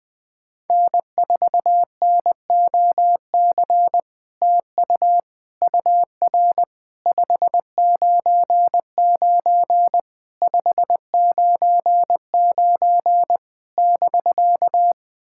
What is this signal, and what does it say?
Morse code, heard as a keyed tone, about 21 wpm, 700 Hz. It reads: N4NOC TU UR 599 599 <BK>
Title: SOTA QSO Practice 20wpm